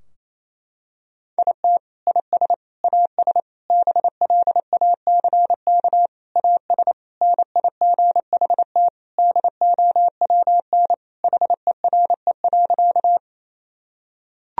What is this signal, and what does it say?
Morse code, heard as a keyed tone, about 28 wpm, 700 Hz. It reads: IT IS AS BLACK AS NIGHT DOWN HERE.
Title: Sentences from Top 1000 Words - Review Part 2 - 28wpm